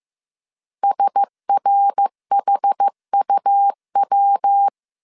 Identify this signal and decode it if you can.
Morse code, heard as a keyed tone, about 15 wpm, 800 Hz. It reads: SRHUW
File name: CW / DET / 1 6 1500C S B 14wpm 2958